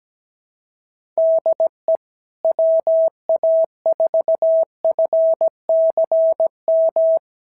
Morse code, heard as a keyed tone, about 17 wpm, 650 Hz. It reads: DE WA4FCM
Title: SOTA QSO Practice 17wpm